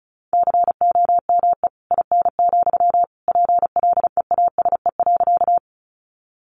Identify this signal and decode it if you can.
Morse code, heard as a keyed tone, about 34 wpm, 700 Hz. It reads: COME IN, PLEASE.
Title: Sentences from Top 1500 Words 35wpm